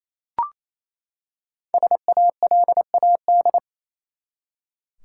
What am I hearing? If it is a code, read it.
Morse code, heard as a keyed tone, about 28 wpm, 700 Hz. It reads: SALAD
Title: Five Letter Words - 28wpm